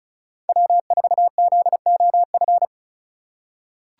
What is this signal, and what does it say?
Morse code, heard as a keyed tone, about 36 wpm, 700 Hz. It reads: W4ZOF
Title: Code Groups - 5 Letter:Numbers - 35wpm